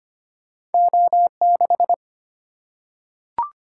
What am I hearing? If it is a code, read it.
Morse code, heard as a keyed tone, about 25 wpm, 700 Hz. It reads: O 6
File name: Two Letter-Number Combinations - 25wpm